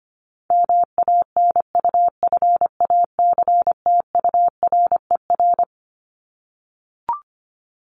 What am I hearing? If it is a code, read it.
Morse code, heard as a keyed tone, about 25 wpm, 700 Hz. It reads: MANUFACTURER
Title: Top 1751-2000 Words 25wpm